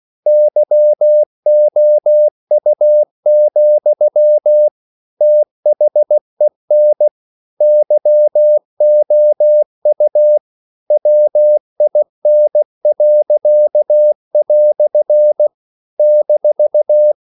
Morse code, heard as a keyed tone, about 16 wpm, 600 Hz. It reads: YOU, THEN YOU WIN." -
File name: CW / Practice / EnglishQuotations0216